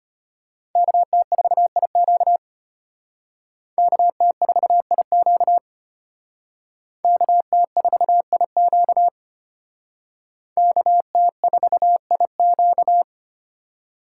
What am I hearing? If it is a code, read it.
Morse code, heard as a keyed tone, about 33 wpm, 700 Hz. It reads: KT4IQ KT4IQ KT4IQ KT4IQ